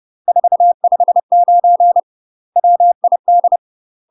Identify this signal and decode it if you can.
Morse code, heard as a keyed tone, about 30 wpm, 700 Hz. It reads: 459 WID
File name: UK MASTER QSOs TRAINING / 30WPM / UK MASTER QSOs 0002